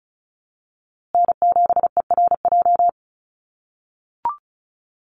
Morse code, heard as a keyed tone, about 34 wpm, 700 Hz. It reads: N7ERJ
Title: USA Call Signs Repeated 2x 35wpm